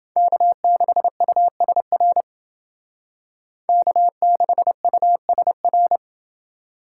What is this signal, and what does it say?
Morse code, heard as a keyed tone, about 29 wpm, 700 Hz. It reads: K6USR K6USR